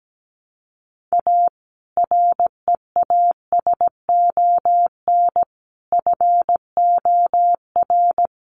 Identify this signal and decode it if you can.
Morse code, heard as a keyed tone, about 17 wpm, 700 Hz. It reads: A REASON FOR
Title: Sentences from Top 1300 Words 17wpm